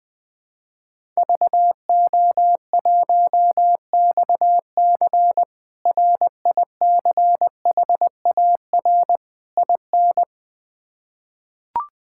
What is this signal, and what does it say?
Morse code, heard as a keyed tone, about 20 wpm, 700 Hz. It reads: VO1XC RICHAR IN
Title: K1USN SST QSO Practice - 20wpm